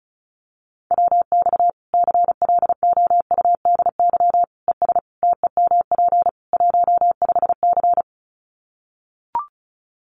Morse code, heard as a keyed tone, about 34 wpm, 700 Hz. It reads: WX CLOUDY ES TEMP 15C
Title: Common QSO Phrases 35wpm